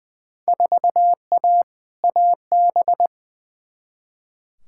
Morse code, heard as a keyed tone, about 20 wpm, 700 Hz. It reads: 4A AB
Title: Field Day Exchanges Encore - 20wpm